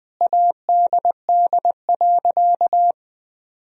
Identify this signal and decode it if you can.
Morse code, heard as a keyed tone, about 20 wpm, 700 Hz. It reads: ADD.
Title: Sentences from Top 1000 Words - Review Part 1 - 20wpm